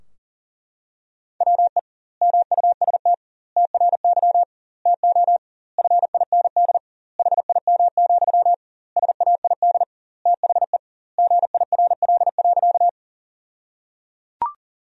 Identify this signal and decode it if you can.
Morse code, heard as a keyed tone, about 37 wpm, 700 Hz. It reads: WE MUST TRY TO FIND HIM, SAID THE GIRL.
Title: 2x Word-Spacing - Sentences from Top 400 Words - 40wpm